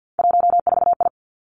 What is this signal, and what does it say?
Morse code, heard as a keyed tone, about 50 wpm, 700 Hz. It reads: 14I
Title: Three Challenging Letter-Number Combinations 50wpm